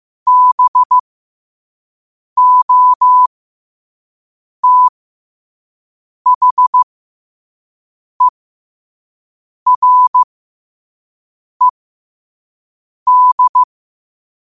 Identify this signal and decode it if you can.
Morse code, heard as a keyed tone, about 15 wpm, spaced as for 5 wpm, 1000 Hz. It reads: BOTHERED
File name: qotdfiles / 5wpm